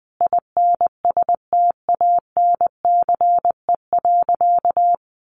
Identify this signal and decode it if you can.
Morse code, heard as a keyed tone, about 20 wpm, 700 Hz. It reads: INSTANCE.
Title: Sentences from Top 800 Words 20wpm